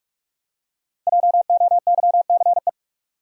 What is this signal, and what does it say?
Morse code, heard as a keyed tone, about 45 wpm, 700 Hz. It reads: JOYCE
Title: USA Common Names - Max 5 Letters - 45wpm